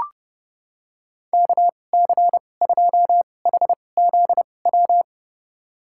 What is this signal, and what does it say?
Morse code, heard as a keyed tone, about 30 wpm, spaced as for 22 wpm, 700 Hz. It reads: KC2HZW
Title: USA Call Signs - 6 Characters - 2x Character Spacing - 30wpm